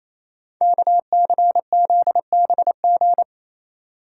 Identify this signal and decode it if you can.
Morse code, heard as a keyed tone, about 27 wpm, 690 Hz. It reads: KCZBG
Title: Code Groups - 5 Letters - 28wpm